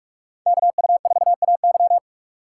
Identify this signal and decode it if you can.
Morse code, heard as a keyed tone, about 56 wpm, 700 Hz. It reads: KU4AY
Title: Speed Racing - USA 2x2 Call Signs 45wpm